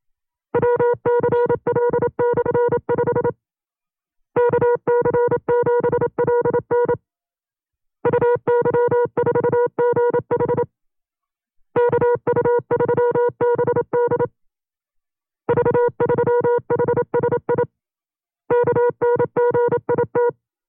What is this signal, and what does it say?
Morse code, heard as a keyed tone, about 27 wpm, 480 Hz. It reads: WCL/5 KC7LN UY4GH KU3BD V3HSI KNGIT
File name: CW-practice/25WPM